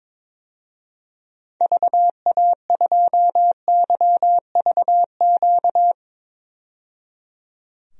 Morse code, heard as a keyed tone, about 22 wpm, 700 Hz. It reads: VA2YVQ
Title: DXCC-100 Call Signs - 6 Characters - 22wpm